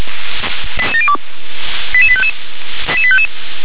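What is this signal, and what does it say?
Five-tone selective call (ZVEI), heard as quick runs of five tones, tones of about 70 ms. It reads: 88702 88855
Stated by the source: ZVEI five-tone selective call 88702 received on an SDR.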